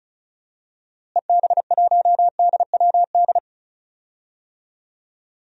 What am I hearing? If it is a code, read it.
Morse code, heard as a keyed tone, about 34 wpm, 700 Hz. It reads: EB1DWD
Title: DXCC-100 Call Signs - 6 Characters - Repeated 3x - 35wpm